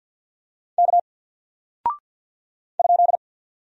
Morse code, heard as a keyed tone, about 48 wpm, 700 Hz. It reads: K ?
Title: Single Character (Letter:Number:Punctuation:Prosign) Alt 50wpm